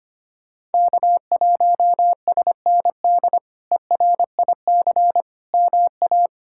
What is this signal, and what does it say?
Morse code, heard as a keyed tone, about 25 wpm, 700 Hz. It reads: K1SND ERIC MA
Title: K1USN SST QSO Practice - 25wpm